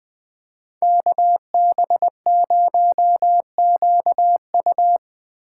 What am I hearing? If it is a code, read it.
Morse code, heard as a keyed tone, about 20 wpm, 700 Hz. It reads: KB0QU
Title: Speed Racing - USA Call Signs 17wpm